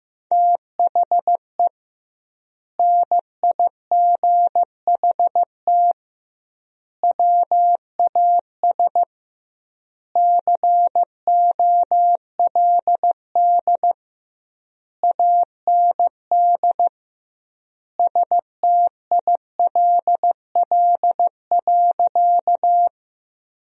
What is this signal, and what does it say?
Morse code, heard as a keyed tone, about 15 wpm, 700 Hz. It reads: THE NIGHT WAS COLD AND STILL.
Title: 2x Word-Spacing - Sentences from Top 400 Words - 15wpm